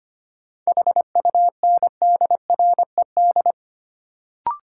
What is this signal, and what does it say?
Morse code, heard as a keyed tone, about 25 wpm, 700 Hz. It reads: HUNDRED
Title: Sentences from Top 400 Words - New Word Drill - 25wpm